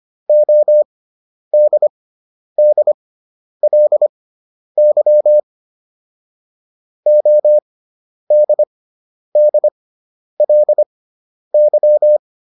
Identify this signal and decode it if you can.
Morse code, heard as a keyed tone, about 25 wpm, spaced as for 10 wpm, 600 Hz. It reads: ODDLY ODDLY